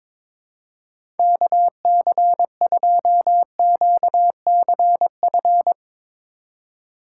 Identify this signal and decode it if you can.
Morse code, heard as a keyed tone, about 22 wpm, 700 Hz. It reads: KC2QCF
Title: USA Call Signs - 2x Character Spacing - 22wpm